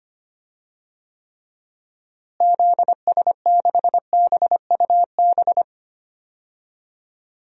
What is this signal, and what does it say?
Morse code, heard as a keyed tone, about 25 wpm, 700 Hz. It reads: ZS6BUB
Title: DXCC-100 Call Signs - 6 Characters - Repeated 2x - 25wpm